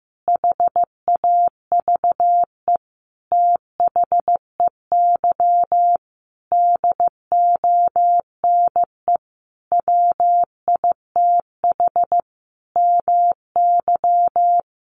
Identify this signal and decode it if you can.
Morse code, heard as a keyed tone, about 15 wpm, 700 Hz. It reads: HAVE THEY DONE WITH MY
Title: Sentences from Top 500 Words 15wpm